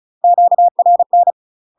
Morse code, heard as a keyed tone, about 34 wpm, 700 Hz. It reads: QRN
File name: UK MASTER QSOs TRAINING / 35WPM / UK MASTER QSOs 0001